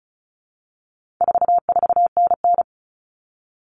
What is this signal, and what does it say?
Morse code, heard as a keyed tone, about 34 wpm, 700 Hz. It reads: 44NN
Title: Common Cut Numbers - 4-Digit 35wpm